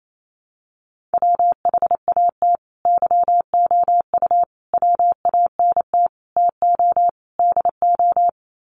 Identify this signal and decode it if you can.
Morse code, heard as a keyed tone, about 28 wpm, 700 Hz. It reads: WHAT YOU WANT TO DO
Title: Sets of 5 Words Encore 28wpm